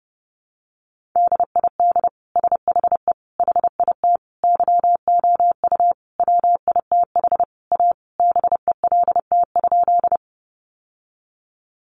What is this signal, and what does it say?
Morse code, heard as a keyed tone, about 30 wpm, 700 Hz. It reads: DID SHE HIT YOU WITH A BELT?
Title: Sentences from Top 1400 Words 30wpm